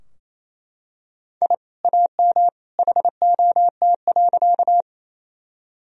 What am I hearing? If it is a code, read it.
Morse code, heard as a keyed tone, about 28 wpm, 700 Hz. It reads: I AM HOT.
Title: Sentences from Top 1000 Words - Review Part 3 - 28wpm